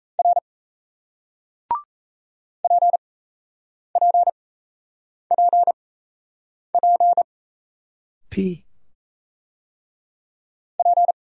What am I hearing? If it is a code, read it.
Morse code, heard as a keyed tone, about 38 wpm, spaced as for 8 wpm, 700 Hz. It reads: R PPPP P